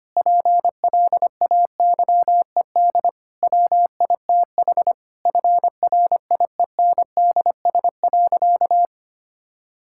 Morse code, heard as a keyed tone, about 25 wpm, 700 Hz. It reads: PLAYED WITH FRIENDS.
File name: practice / 25wpm / Difficult Sentences with Hints - 25wpm